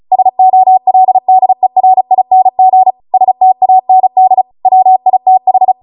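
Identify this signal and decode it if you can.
Morse code, heard as a keyed tone, about 36 wpm, 750 Hz. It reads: SOLDERING STAND WITH
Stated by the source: Amateur radio Morse code practice archive transmission at 35 wpm.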